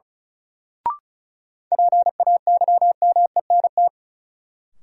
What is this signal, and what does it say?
Morse code, heard as a keyed tone, about 36 wpm, 700 Hz. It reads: PAYMENT
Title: Sentences from Top 1200 Words - New Word Drill - 35wpm